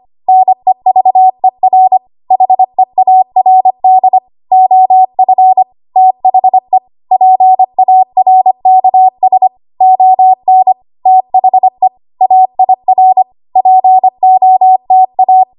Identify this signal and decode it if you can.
Morse code, heard as a keyed tone, about 25 wpm, 750 Hz. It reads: NEVER HEARD OF THE PARKS ON THE AIR POTA
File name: Archive / 25 WPM / 25WPM